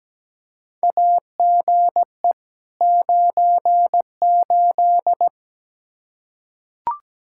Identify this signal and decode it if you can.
Morse code, heard as a keyed tone, about 17 wpm, 700 Hz. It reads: AGE 98
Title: Practice QSO - Exchange 3 - More Info - 17wpm